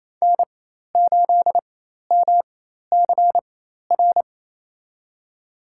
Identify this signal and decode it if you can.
Morse code, heard as a keyed tone, about 27 wpm, spaced as for 13 wpm, 700 Hz. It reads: N8MCR
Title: USA Call Signs - 5 Characters - 4x Character Spacing - 28wpm